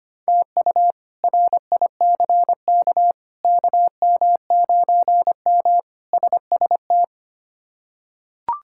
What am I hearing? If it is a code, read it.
Morse code, heard as a keyed tone, about 25 wpm, 700 Hz. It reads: TU RICK KM9M SST
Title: K1USN SST QSO Practice - 25wpm